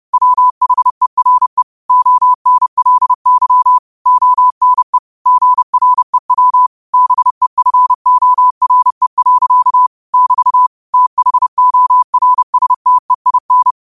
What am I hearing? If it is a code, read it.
Morse code, heard as a keyed tone, about 30 wpm, 1000 Hz. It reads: WHERE ONLY ONE GREW BEFORE. <BT> THORSTEIN
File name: 30wpm